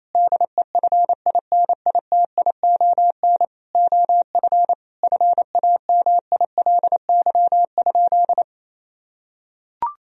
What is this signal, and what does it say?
Morse code, heard as a keyed tone, about 28 wpm, 700 Hz. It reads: DEFINITION OF FAMILY?
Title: Sentences from Top 1000 Words - Review Part 4 - 28wpm